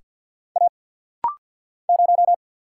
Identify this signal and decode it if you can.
Morse code, heard as a keyed tone, about 50 wpm, 690 Hz. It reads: A 0
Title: Single Letter-Number Alt - Rapid-Fire 50wpm